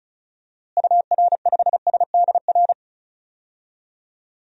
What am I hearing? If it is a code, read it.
Morse code, heard as a keyed tone, about 36 wpm, 700 Hz. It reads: UR5SDR
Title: DXCC-100 Call Signs - 6 Characters - Repeated 3x - 35wpm